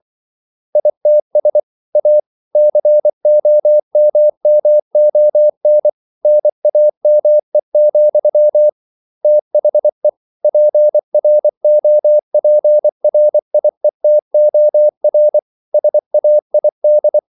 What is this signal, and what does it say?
Morse code, heard as a keyed tone, about 24 wpm, 600 Hz. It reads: ITS A COMMON NAME, THE PROPRIETOR SAID